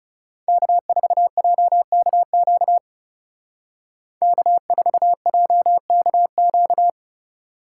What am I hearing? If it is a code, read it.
Morse code, heard as a keyed tone, about 35 wpm, 700 Hz. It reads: K4JKQ K4JKQ
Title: Speed Racing - USA 1x3 Call Signs 30wpm